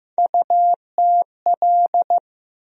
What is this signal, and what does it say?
Morse code, heard as a keyed tone, about 15 wpm, 700 Hz. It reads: UTL
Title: USA Call Sign Postfixes - 3 Letters 15wpm